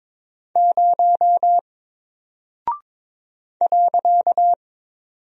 Morse code, heard as a keyed tone, about 22 wpm, 700 Hz. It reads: 0 .
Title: Single Character (Letter:Number:Punctuation:Prosign) Alt 22wpm